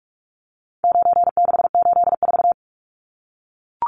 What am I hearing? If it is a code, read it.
Morse code, heard as a keyed tone, about 44 wpm, 700 Hz. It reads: 9684
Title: Numbers - 4-Digit 45wpm